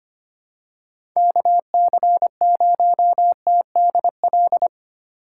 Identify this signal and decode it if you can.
Morse code, heard as a keyed tone, about 25 wpm, 700 Hz. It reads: KC0TDL
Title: USA Call Signs - 8x Character Spacing - 25wpm